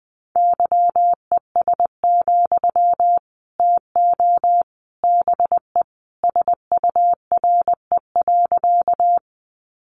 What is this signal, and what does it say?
Morse code, heard as a keyed tone, about 20 wpm, 700 Hz. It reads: YES, TO BE SURE.
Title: Sentences from Top 500 Words 20wpm